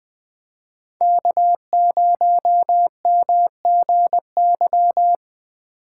Morse code, heard as a keyed tone, about 20 wpm, 700 Hz. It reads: K0MGY
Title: USA Call Signs - 5 Characters - 6x Character Spacing - 20wpm